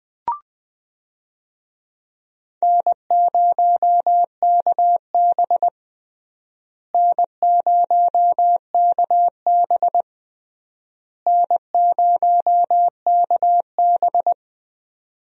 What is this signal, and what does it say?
Morse code, heard as a keyed tone, about 21 wpm, 700 Hz. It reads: N0KB N0KB N0KB
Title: USA Call Signs - 4 Characters - Repeated 3x - 20wpm